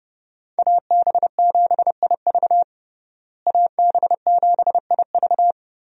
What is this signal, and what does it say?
Morse code, heard as a keyed tone, about 30 wpm, 700 Hz. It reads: AB7IV AB7IV
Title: USA Call Signs - 5 Characters - Repeated 3x - 30wpm